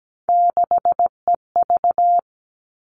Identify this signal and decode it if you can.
Morse code, heard as a keyed tone, about 17 wpm, 700 Hz. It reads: 6EV
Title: All Cut Numbers - 3-Digit 17wpm